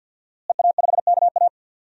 Morse code, heard as a keyed tone, about 53 wpm, 700 Hz. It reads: EA5KA
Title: DXCC-100 Call Signs - 5 Characters - Repeated 3x - 50wpm